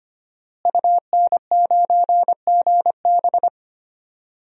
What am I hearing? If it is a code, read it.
Morse code, heard as a keyed tone, about 25 wpm, 700 Hz. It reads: UN9GB
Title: DXCC-100 Call Signs - 5 Characters - Repeated 2x - 25wpm